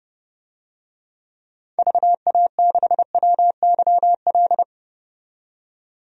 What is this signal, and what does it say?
Morse code, heard as a keyed tone, about 30 wpm, 700 Hz. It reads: VA6WYL
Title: DXCC-100 Call Signs - 6 Characters - 30wpm